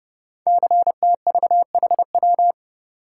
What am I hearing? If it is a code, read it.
Morse code, heard as a keyed tone, about 30 wpm, 700 Hz. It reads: CTVHW
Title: Code Groups - 5 Letters - 30wpm